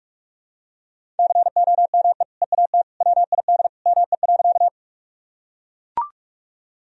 Morse code, heard as a keyed tone, about 41 wpm, 700 Hz. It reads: COME EAT WID ME.
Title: Sentences from Top 300 Words and Abbreviations 45wpm